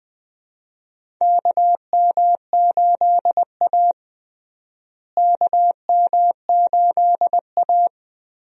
Morse code, heard as a keyed tone, about 21 wpm, 700 Hz. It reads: KM8A KM8A
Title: USA Call Signs - 4 Characters - Repeated 3x - 20wpm